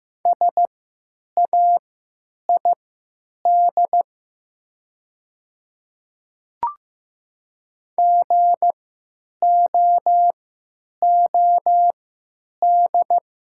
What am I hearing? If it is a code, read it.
Morse code, heard as a keyed tone, about 15 wpm, spaced as for 9 wpm, 700 Hz. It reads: SAID GOOD
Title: 3x Character Spacing - Top 200 Words - 15wpm